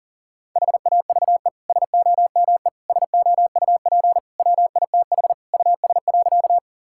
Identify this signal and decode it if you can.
Morse code, heard as a keyed tone, about 40 wpm, 690 Hz. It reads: HAVE SOME SOUP WITH US.
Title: Sentences from Top 1000 Words - Review Part 6 - 40wpm